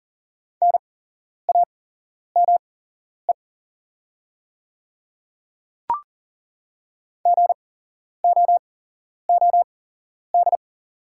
Morse code, heard as a keyed tone, about 40 wpm, spaced as for 11 wpm, 690 Hz. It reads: NAME GOOD
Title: 8x Character Spacing - Top 200 Words - 40wpm